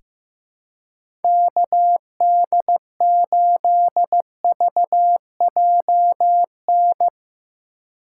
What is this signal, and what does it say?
Morse code, heard as a keyed tone, about 14 wpm, 700 Hz. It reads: KD8VJN
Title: USA Call Signs - 6 Characters - 8x Character Spacing - 15wpm